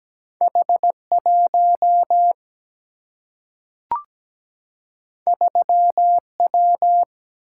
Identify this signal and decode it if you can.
Morse code, heard as a keyed tone, about 17 wpm, 700 Hz. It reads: H 1 3 W
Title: Single-Letter Single-Number 17wpm